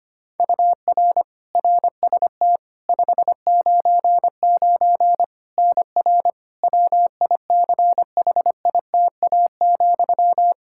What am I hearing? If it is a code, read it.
Morse code, heard as a keyed tone, about 25 wpm, 700 Hz. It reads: UR RST 599 NR WICHITA,
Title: K1USN SST QSO Practice 25wpm